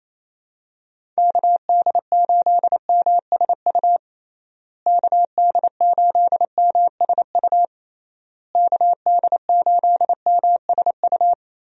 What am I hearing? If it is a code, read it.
Morse code, heard as a keyed tone, about 28 wpm, 700 Hz. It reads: KD8MSU KD8MSU KD8MSU
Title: USA Call Signs - 6 Characters - Repeated 3x - 28wpm